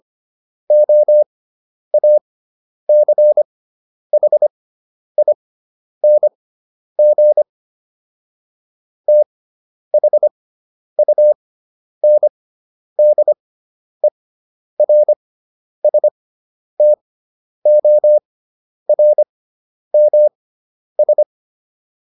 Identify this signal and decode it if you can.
Morse code, heard as a keyed tone, about 25 wpm, spaced as for 10 wpm, 600 Hz. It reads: OACHING THUNDERSTORMS